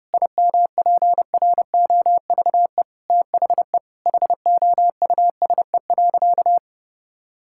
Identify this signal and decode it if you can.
Morse code, heard as a keyed tone, about 32 wpm, 700 Hz. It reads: IMPROVE THE HOUSE.
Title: Sentences from Top 1000 Words - Review Part 5 - 30wpm